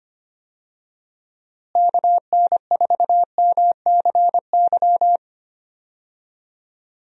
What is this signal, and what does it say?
Morse code, heard as a keyed tone, about 25 wpm, 700 Hz. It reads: KN4MCY
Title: USA Call Signs - 6 Characters - Repeated 2x - 25wpm